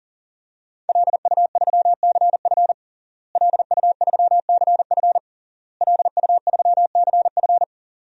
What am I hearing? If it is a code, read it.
Morse code, heard as a keyed tone, about 40 wpm, 700 Hz. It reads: LU3CF LU3CF LU3CF
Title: DXCC-100 Call Signs - 5 Characters - Repeated 3x - 40wpm